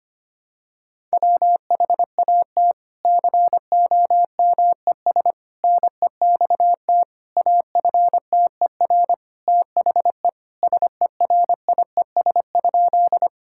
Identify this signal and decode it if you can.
Morse code, heard as a keyed tone, about 25 wpm, 700 Hz. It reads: WHAT COMES NEXT AFTER THE SERIES?
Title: Sentences from Top 1000 Words - Review Part 5 - 25wpm